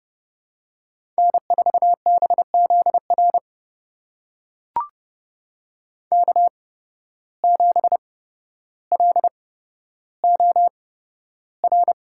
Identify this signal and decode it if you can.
Morse code, heard as a keyed tone, about 30 wpm, spaced as for 13 wpm, 700 Hz. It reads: N4BZR K7LOR
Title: USA Call Signs - 5 Characters - 8x Character Spacing - 30wpm